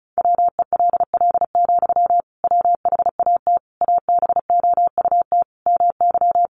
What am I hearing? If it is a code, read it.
Morse code, heard as a keyed tone, about 34 wpm, 700 Hz. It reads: WELL, WHAT ABOUT MY
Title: Sentences from Top 500 Words 35wpm